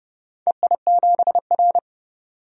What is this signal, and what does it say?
Morse code, heard as a keyed tone, about 30 wpm, 700 Hz. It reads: EI7R
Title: DXCC-100 Call Signs - 4 Characters - 30wpm